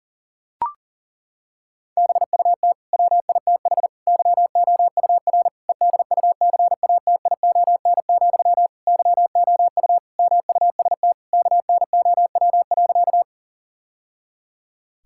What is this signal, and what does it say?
Morse code, heard as a keyed tone, about 40 wpm, 700 Hz. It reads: BUT WITH YOUR EDUCATION, YOU MUST KNOW.
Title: Sentences from Top 1000 Words - Review Part 6 - 40wpm